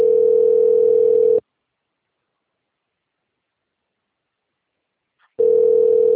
Ringback tone (North America), 440 + 480 Hz.